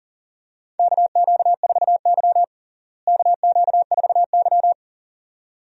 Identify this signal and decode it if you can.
Morse code, heard as a keyed tone, about 37 wpm, 700 Hz. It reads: KQ4Y KQ4Y
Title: USA Call Signs - 4 Characters - Repeated 2x - 40wpm